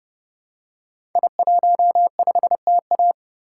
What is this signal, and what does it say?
Morse code, heard as a keyed tone, about 30 wpm, 700 Hz. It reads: I15TA
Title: Code Groups - 5 Letter:Numbers - 30wpm